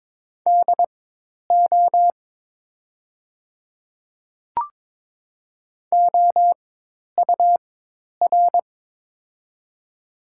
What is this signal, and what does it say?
Morse code, heard as a keyed tone, about 22 wpm, spaced as for 10 wpm, 700 Hz. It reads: DO OUR
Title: 4x Character Spacing - Top 300 Words - 22wpm